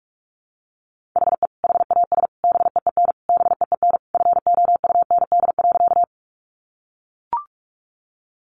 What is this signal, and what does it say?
Morse code, heard as a keyed tone, about 44 wpm, 700 Hz. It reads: HE HAS BEEN BEEN FOUND.
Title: Sentences from Top 300 Words 45wpm